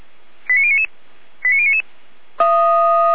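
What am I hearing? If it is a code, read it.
Five-tone selective call (ZVEI), heard as quick runs of five tones, tones of about 70 ms. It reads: 89099 89099
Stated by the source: ZVEI five-tone selective call received on an SDR at 455.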